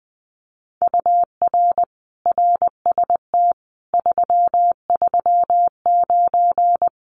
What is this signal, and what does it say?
Morse code, heard as a keyed tone, about 20 wpm, 700 Hz. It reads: UR RST 339
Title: Common QSO Phrases Encore 20wpm